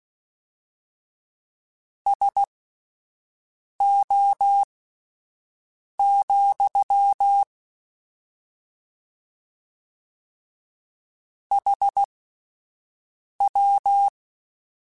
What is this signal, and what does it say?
Morse code, heard as a keyed tone, about 16 wpm, spaced as for 5 wpm, 800 Hz. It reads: SO, HW
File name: (spaced) / QSOs Spaced-mp3s / WD0BGP 1 @10 sp